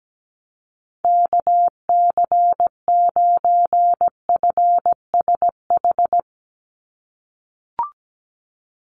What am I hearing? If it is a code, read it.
Morse code, heard as a keyed tone, about 17 wpm, 700 Hz. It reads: KC9FSH
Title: USA Call Signs 17wpm